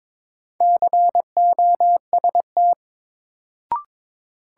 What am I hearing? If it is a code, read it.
Morse code, heard as a keyed tone, about 21 wpm, 700 Hz. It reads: COST
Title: Sets of 3 Words - New Word Drill - 22wpm